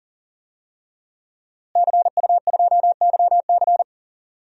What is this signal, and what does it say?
Morse code, heard as a keyed tone, about 40 wpm, 700 Hz. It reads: CU2YC
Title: DXCC-100 Call Signs - 5 Characters - Repeated 2x - 40wpm